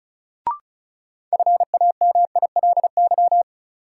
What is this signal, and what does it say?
Morse code, heard as a keyed tone, about 36 wpm, 700 Hz. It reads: FAMILY
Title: Sentences from Top 400 Words - New Word Drill - 35wpm